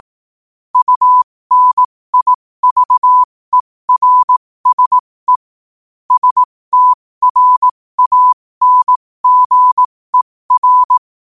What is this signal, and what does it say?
Morse code, heard as a keyed tone, about 18 wpm, spaced as for 15 wpm, 1000 Hz. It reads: UNIVERSE STRANGER
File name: qotdfiles / 15wpm